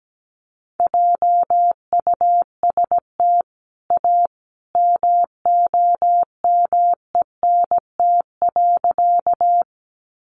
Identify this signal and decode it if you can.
Morse code, heard as a keyed tone, about 17 wpm, 700 Hz. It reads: JUST A MOMENT.
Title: Sentences from Top 1000 Words 17wpm